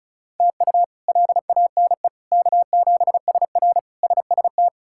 Morse code, heard as a keyed tone, about 36 wpm, 700 Hz. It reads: TU LANE K7SR SST
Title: K1USN SST QSO Practice - 35wpm